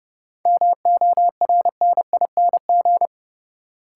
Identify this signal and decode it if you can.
Morse code, heard as a keyed tone, about 32 wpm, 700 Hz. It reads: MORNING
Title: Sentences from Top 400 Words - New Word Drill - 30wpm